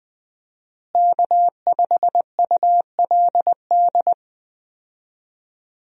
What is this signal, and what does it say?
Morse code, heard as a keyed tone, about 20 wpm, 700 Hz. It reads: K5ULD
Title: USA Call Signs - 5 Characters - Repeated 2x - 20wpm